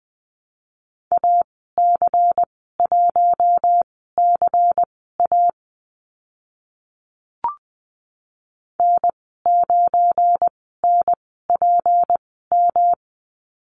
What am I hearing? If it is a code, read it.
Morse code, heard as a keyed tone, about 20 wpm, spaced as for 15 wpm, 700 Hz. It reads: AC1CA N9NPM
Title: 186 Lesson 30g - Letter b - Call Signs 20wpm